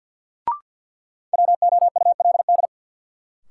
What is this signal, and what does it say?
Morse code, heard as a keyed tone, about 48 wpm, 700 Hz. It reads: WOULD
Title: Sentences from Top 100 Words - New Word Drill - 50wpm